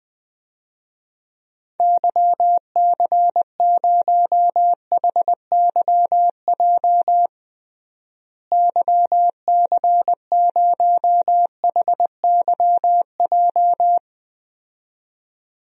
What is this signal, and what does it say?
Morse code, heard as a keyed tone, about 20 wpm, 700 Hz. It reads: YC0HYJ YC0HYJ
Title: DXCC-100 Call Signs - 6 Characters - Repeated 2x - 20wpm